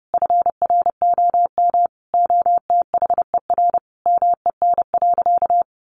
Morse code, heard as a keyed tone, about 30 wpm, 700 Hz. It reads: FROM OTHER MEN.
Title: Sentences from Top 1100 Words 30wpm